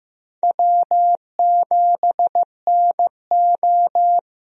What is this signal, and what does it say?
Morse code, heard as a keyed tone, about 15 wpm, 700 Hz. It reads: W7NO
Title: USA Call Signs - 4 Characters - Repeated 2x - 15wpm